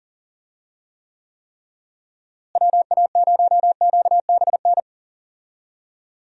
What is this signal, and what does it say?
Morse code, heard as a keyed tone, about 40 wpm, 700 Hz. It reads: WA0QBN
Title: USA Call Signs - 6 Characters - Repeated 3x - 40wpm